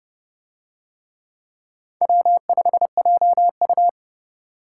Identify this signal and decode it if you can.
Morse code, heard as a keyed tone, about 29 wpm, 700 Hz. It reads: W5JU